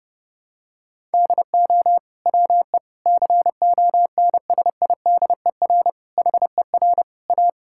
Morse code, heard as a keyed tone, about 32 wpm, 700 Hz. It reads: DO WE CONSIDER HER A
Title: Sentences from Top 1000 Words - Review Part 5 - 30wpm